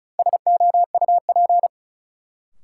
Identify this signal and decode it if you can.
Morse code, heard as a keyed tone, about 34 wpm, 700 Hz. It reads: SOUP
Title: Sentences from Top 1600 Words - New Word Drill Alt - 35wpm